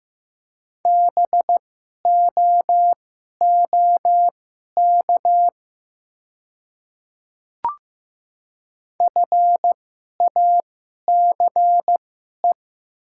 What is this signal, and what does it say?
Morse code, heard as a keyed tone, about 15 wpm, spaced as for 11 wpm, 700 Hz. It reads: BOOK FACE
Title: 2x Character Spacing - Top 300 Words - 15wpm